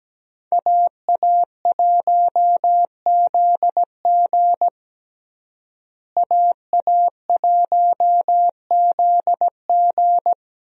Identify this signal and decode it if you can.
Morse code, heard as a keyed tone, about 17 wpm, 700 Hz. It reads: AA1ZG AA1ZG
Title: USA Call Signs - 5 Characters - Repeated 3x - 17wpm